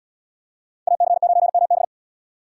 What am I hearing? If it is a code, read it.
Morse code, heard as a keyed tone, about 74 wpm, 700 Hz. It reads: AC9MB